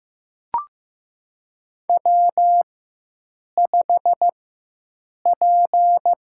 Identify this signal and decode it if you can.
Morse code, heard as a keyed tone, about 15 wpm, 700 Hz. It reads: W 5 P
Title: USA Call Signs - 4 Characters - 4x Character Spacing - 15wpm